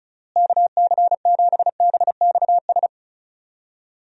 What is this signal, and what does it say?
Morse code, heard as a keyed tone, about 34 wpm, 700 Hz. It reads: KC7BXS
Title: USA Call Signs - 6 Characters - Repeated 3x - 35wpm